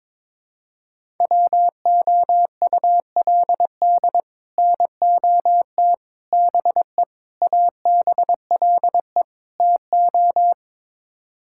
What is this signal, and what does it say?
Morse code, heard as a keyed tone, about 22 wpm, 700 Hz. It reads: WOULD NOT BE ABLE TO
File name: practice / 22wpm / Build A Phrase - 5-Word Phrases - 22wpm